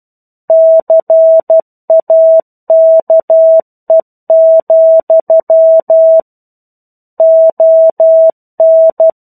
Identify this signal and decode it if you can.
Morse code, heard as a keyed tone, about 12 wpm, 650 Hz. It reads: CAKE, ON
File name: 01-Chapter-01 12 WPM